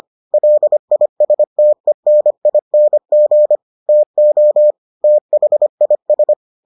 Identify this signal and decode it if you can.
Morse code, heard as a keyed tone, about 25 wpm, 600 Hz. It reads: LISTENING TO THIS